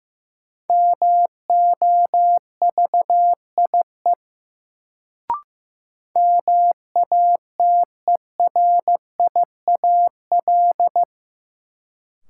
Morse code, heard as a keyed tone, about 15 wpm, 700 Hz. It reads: MOVIE MATERIAL
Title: Sentences from Top 1000 Words - Review Part 5 - New Word Drill - 15wpm